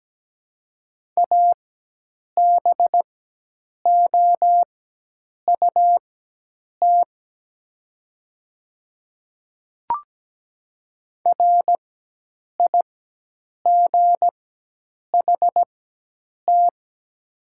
Morse code, heard as a keyed tone, about 17 wpm, spaced as for 8 wpm, 700 Hz. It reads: ABOUT RIGHT
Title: 4x Character Spacing - Top 300 Words - 17wpm